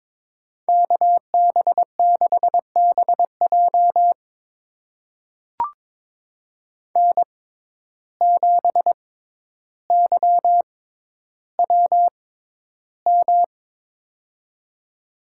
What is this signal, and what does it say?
Morse code, heard as a keyed tone, about 22 wpm, spaced as for 11 wpm, 700 Hz. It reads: KB6BJ N7YWM
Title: USA Call Signs - 6x Character Spacing - 22wpm